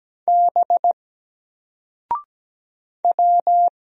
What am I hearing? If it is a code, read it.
Morse code, heard as a keyed tone, about 17 wpm, 700 Hz. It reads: B W
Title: Single Letter-Number 17wpm